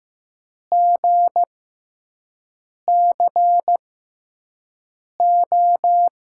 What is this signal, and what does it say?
Morse code, heard as a keyed tone, about 15 wpm, spaced as for 5 wpm, 700 Hz. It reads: GCO
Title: USA Call Signs - 6 Characters - 6x Character Spacing - 15wpm